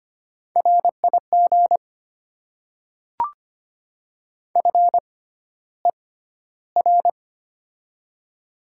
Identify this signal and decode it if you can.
Morse code, heard as a keyed tone, about 25 wpm, spaced as for 13 wpm, 700 Hz. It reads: RIG FER
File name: Top 20 QSO Elements - 6x Character Spacing - 25wpm